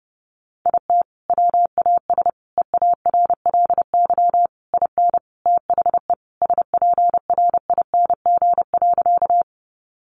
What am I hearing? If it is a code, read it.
Morse code, heard as a keyed tone, about 30 wpm, 700 Hz. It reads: IT WAS EARLY IN THE SPRING.
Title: Sentences from Top 1200 Words 30wpm